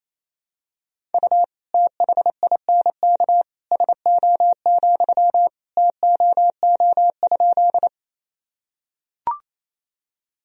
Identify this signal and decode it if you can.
Morse code, heard as a keyed tone, about 28 wpm, 700 Hz. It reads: U THINK SO, TOO?
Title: Sentences from Top 200 Words and Abbreviations 28wpm